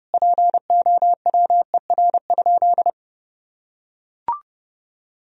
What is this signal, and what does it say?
Morse code, heard as a keyed tone, about 32 wpm, 700 Hz. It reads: POWER?
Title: Sentences from Top 1000 Words - Review Part 2 - 30wpm